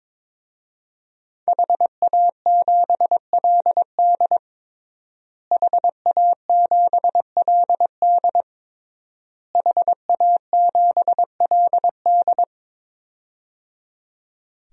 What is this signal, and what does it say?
Morse code, heard as a keyed tone, about 22 wpm, 700 Hz. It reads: HA7LD HA7LD HA7LD
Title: DXCC-100 Call Signs - 5 Characters - Repeated 3x - 22wpm